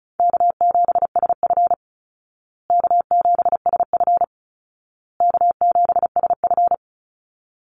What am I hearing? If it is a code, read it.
Morse code, heard as a keyed tone, about 34 wpm, 700 Hz. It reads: K7SF K7SF K7SF
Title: Easy Call Signs Repeated 3x - 35wpm